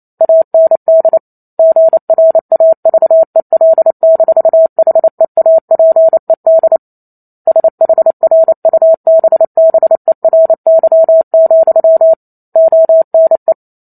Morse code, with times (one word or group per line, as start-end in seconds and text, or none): 0.2-1.2 AND
1.6-6.8 GRAVEL-HEAPED
7.5-12.1 SHRUBBERY,
12.6-13.5 ONE